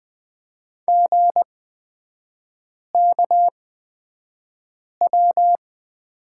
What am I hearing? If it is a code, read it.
Morse code, heard as a keyed tone, about 20 wpm, spaced as for 5 wpm, 690 Hz. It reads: GKW
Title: 229 Lesson 37e - Letter x - Three Characters 20wpm